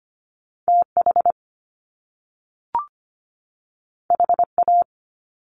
Morse code, heard as a keyed tone, about 25 wpm, 700 Hz. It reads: TH HA
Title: Top Two Letter Combinations in Words 25wpm